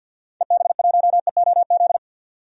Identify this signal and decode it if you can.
Morse code, heard as a keyed tone, about 53 wpm, 700 Hz. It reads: EB1EOZ
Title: DXCC-100 Call Signs - 6 Characters - 50wpm